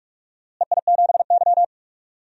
Morse code, heard as a keyed tone, about 41 wpm, 700 Hz. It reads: EI7Y
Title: DXCC-100 Call Signs - 4 Characters - Repeated 2x - 45wpm